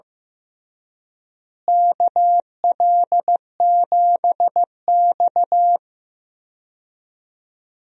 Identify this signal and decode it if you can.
Morse code, heard as a keyed tone, about 15 wpm, 700 Hz. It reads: KL7X
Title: DXCC-100 Call Signs - 4 Characters - Repeated 2x - 15wpm